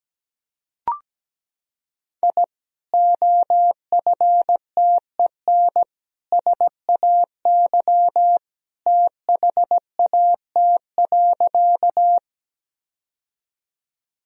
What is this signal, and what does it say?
Morse code, heard as a keyed tone, about 17 wpm, 700 Hz. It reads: I OFTEN SAY THAT.
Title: Sentences from Top 1000 Words - Review Part 1 - 17wpm